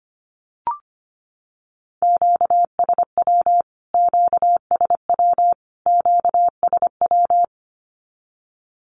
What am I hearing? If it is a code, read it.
Morse code, heard as a keyed tone, about 25 wpm, 700 Hz. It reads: QSW QSW QSW
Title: Q-Codes Repeated 3x 25wpm